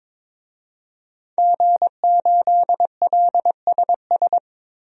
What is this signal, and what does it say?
Morse code, heard as a keyed tone, about 22 wpm, 700 Hz. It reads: G8LSS
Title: DXCC-100 Call Signs - 5 Characters - 22wpm